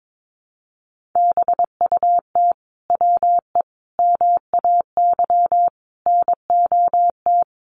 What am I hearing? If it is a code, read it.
Morse code, heard as a keyed tone, about 22 wpm, 700 Hz. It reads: BUT WE MAY NOT
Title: Sentences from Top 1500 Words 22wpm